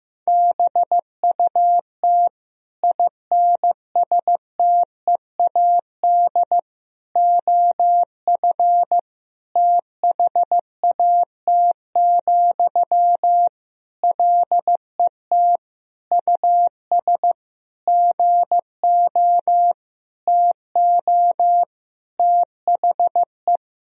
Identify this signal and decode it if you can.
Morse code, heard as a keyed tone, about 15 wpm, 700 Hz. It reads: BUT INSTEAD OF THAT, LET US GO TO THE
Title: Sentences from Top 1000 Words - Review Part 4 - 15wpm